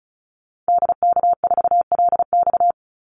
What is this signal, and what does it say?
Morse code, heard as a keyed tone, about 34 wpm, 700 Hz. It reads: DK4LX
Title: Call Signs - 35wpm